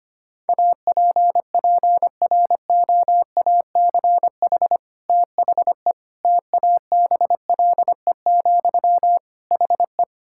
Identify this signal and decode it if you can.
Morse code, heard as a keyed tone, about 25 wpm, 700 Hz. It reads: APPROACH THE TABLE, HE
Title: Sentences from Top 1000 Words - Review Part 6 - 25wpm